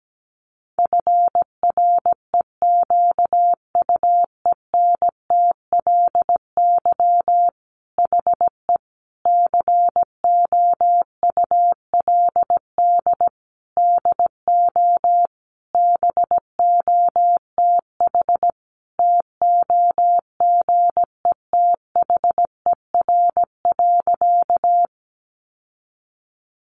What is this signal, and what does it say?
Morse code, heard as a keyed tone, about 17 wpm, 700 Hz. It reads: FREQUENTLY HE COULD DO BOTH TOGETHER.
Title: Sentences from Top 1300 Words 17wpm